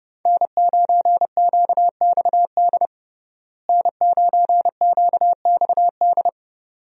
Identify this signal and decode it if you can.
Morse code, heard as a keyed tone, about 30 wpm, 700 Hz. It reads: N9QXD N9QXD
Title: USA Call Signs - 5 Characters - Repeated 2x - 30wpm